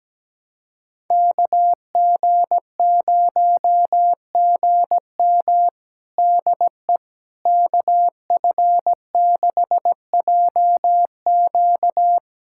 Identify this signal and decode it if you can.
Morse code, heard as a keyed tone, about 17 wpm, 700 Hz. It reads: KG0GM DE KF6JQ
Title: USA Callsign DE Callsign - 5 Characters - 17wpm